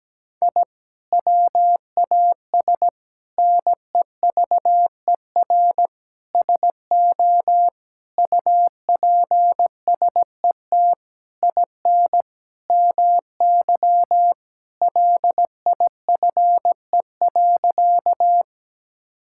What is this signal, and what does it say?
Morse code, heard as a keyed tone, about 17 wpm, 700 Hz. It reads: I WAS NEVER SO UPSET IN MY LIFE.
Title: Sentences from Top 1000 Words - Review Part 5 - 17wpm